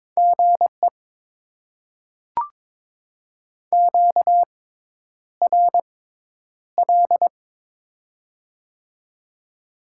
Morse code, heard as a keyed tone, about 22 wpm, spaced as for 10 wpm, 700 Hz. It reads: GE QRL
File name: Top 20 QSO Elements - 6x Character Spacing - 22wpm